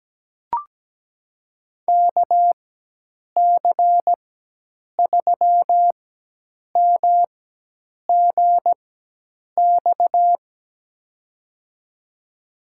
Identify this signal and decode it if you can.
Morse code, heard as a keyed tone, about 17 wpm, spaced as for 8 wpm, 700 Hz. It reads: KC3MGX
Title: USA Call Signs - 6 Characters - 4x Character Spacing - 17wpm